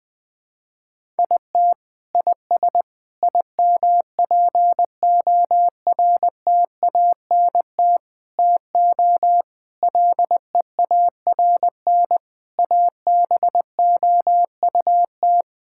Morse code, heard as a keyed tone, about 20 wpm, 700 Hz. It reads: IT IS IMPORTANT TO LEARN ABOUT
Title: Sentences from Top 1000 Words - Review Part 6 - 20wpm